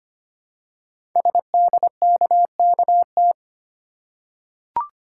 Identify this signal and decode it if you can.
Morse code, heard as a keyed tone, about 25 wpm, 700 Hz. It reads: SDKKT
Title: Code Groups - 5 Letters - 25wpm